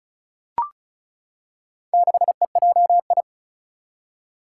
Morse code, heard as a keyed tone, about 36 wpm, 690 Hz. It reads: 6EJI
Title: Four Challenging Letter-Number Combinations 35wpm